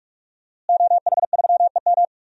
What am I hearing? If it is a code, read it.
Morse code, heard as a keyed tone, about 46 wpm, 700 Hz. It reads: OH3EM
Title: DXCC-100 Call Signs - 5 Characters - Repeated 3x - 45wpm